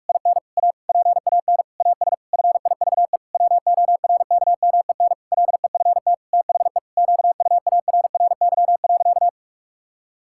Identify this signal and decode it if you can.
Morse code, heard as a keyed tone, about 46 wpm, 700 Hz. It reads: IN A PAN AS FIVE WORKMEN LEFT THE QUARRY.
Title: Pangram Challenge v2 - 45wpm